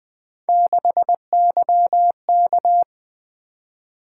Morse code, heard as a keyed tone, about 20 wpm, 700 Hz. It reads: 6YK
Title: Three Letter-Number Combinations - 20wpm